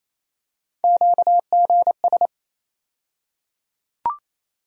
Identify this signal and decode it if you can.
Morse code, heard as a keyed tone, about 28 wpm, 700 Hz. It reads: QGS
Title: USA Call Sign Postfixes - 3 Letters 28wpm